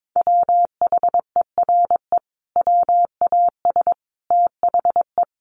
Morse code, heard as a keyed tone, about 22 wpm, 700 Hz. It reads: WHERE WAS THE
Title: Sentences from Top 1000 Words 22wpm